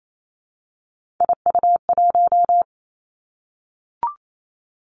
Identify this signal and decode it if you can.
Morse code, heard as a keyed tone, about 28 wpm, 700 Hz. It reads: IU1
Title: Three Challenging Letter-Number Combinations 28wpm